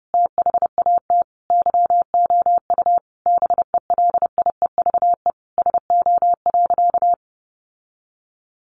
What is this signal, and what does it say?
Morse code, heard as a keyed tone, about 30 wpm, 700 Hz. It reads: THAT YOU BELIEVE SO.
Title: Sentences from Top 1000 Words 30wpm